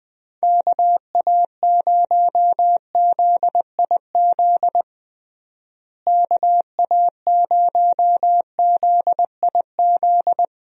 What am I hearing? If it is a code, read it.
Morse code, heard as a keyed tone, about 20 wpm, 700 Hz. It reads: KA0ZIZ KA0ZIZ
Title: USA Call Signs - 6 Characters - Repeated 3x - 20wpm